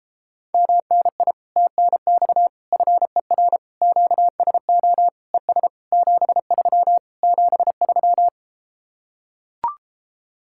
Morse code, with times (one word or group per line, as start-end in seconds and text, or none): 0.5-1.3 MNI
1.6-2.5 TNX
2.7-3.6 FER
3.8-5.1 QSO
5.3-5.7 ES
5.9-7.0 73
7.2-8.3 73